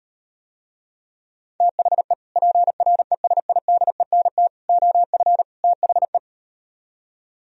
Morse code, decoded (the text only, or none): THE PRESIDENT OF THE